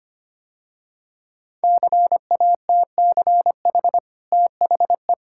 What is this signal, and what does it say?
Morse code, heard as a keyed tone, about 25 wpm, 690 Hz. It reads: CATCH THE
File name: Easy Sentences with Hints - 25wpm